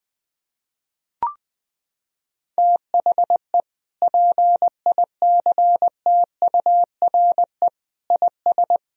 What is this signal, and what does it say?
Morse code, heard as a keyed tone, about 20 wpm, 700 Hz. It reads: THE PICTURE IS
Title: Sentences from Top 1000 Words - Review Part 4 - 20wpm